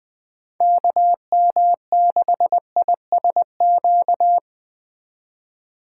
Morse code, heard as a keyed tone, about 20 wpm, 700 Hz. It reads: KM6ISQ
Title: Speed Racing - USA 2x3 Call Signs 17wpm